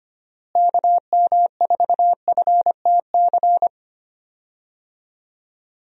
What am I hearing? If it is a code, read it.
Morse code, heard as a keyed tone, about 25 wpm, 700 Hz. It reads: KM4FTC
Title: USA Call Signs - 6 Characters - Repeated 3x - 25wpm